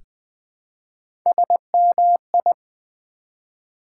Morse code, heard as a keyed tone, about 20 wpm, 700 Hz. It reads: SMI
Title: Build A Word - 1 Letter At A Time - 3 to 5-Letter Words - 20wpm